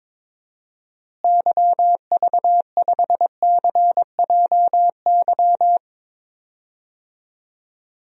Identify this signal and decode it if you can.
Morse code, heard as a keyed tone, about 22 wpm, 700 Hz. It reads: YV5CJY
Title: DXCC-100 Call Signs - 6 Characters - Repeated 2x - 22wpm